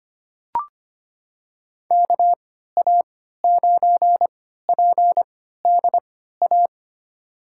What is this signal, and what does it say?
Morse code, heard as a keyed tone, about 25 wpm, spaced as for 14 wpm, 700 Hz. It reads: KA9PDA
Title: USA Call Signs - 3x Character Spacing - 25wpm